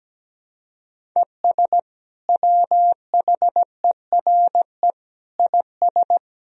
Morse code, heard as a keyed tone, about 17 wpm, 700 Hz. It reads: ES WHERE IS
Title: Sentences from Top 200 Words and Abbreviations 17wpm